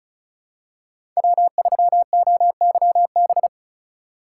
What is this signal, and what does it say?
Morse code, heard as a keyed tone, about 36 wpm, 700 Hz. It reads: W3OYB
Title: USA Call Signs - 5 Characters - 3x Character Spacing - 35wpm